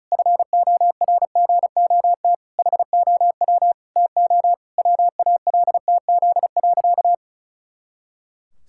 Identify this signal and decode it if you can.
Morse code, heard as a keyed tone, about 34 wpm, 700 Hz. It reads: FORGOT HOW TO WALTZ.
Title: Pangram Challenge v2 - 35wpm